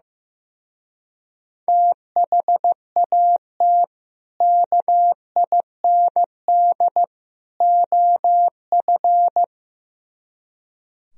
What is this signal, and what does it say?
Morse code, heard as a keyed tone, about 15 wpm, 700 Hz. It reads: THAT KIND OF
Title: Speed-Racing - Sets of 3 Words 15wpm